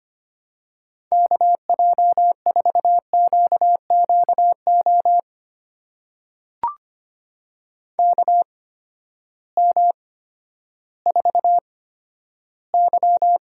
Morse code, heard as a keyed tone, about 25 wpm, spaced as for 12 wpm, 700 Hz. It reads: KJ4QQO KM4Y
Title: USA Call Signs - 6 Characters - 8x Character Spacing - 25wpm